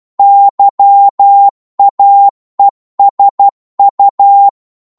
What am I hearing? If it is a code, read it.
Morse code, heard as a keyed tone, about 12 wpm, 800 Hz. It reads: YAESU
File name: Morse / qso-12wpm